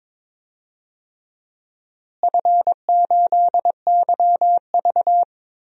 Morse code, heard as a keyed tone, about 22 wpm, 700 Hz. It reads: F8YV
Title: DXCC-100 Call Signs - 4 Characters - Repeated 3x - 22wpm